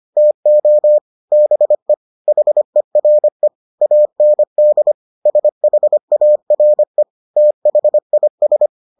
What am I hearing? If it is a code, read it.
Morse code, heard as a keyed tone, about 25 wpm, 600 Hz. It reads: TO BE HERE AND SHARE THIS